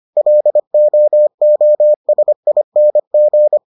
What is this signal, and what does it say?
Morse code, heard as a keyed tone, about 25 wpm, 600 Hz. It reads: LOOSING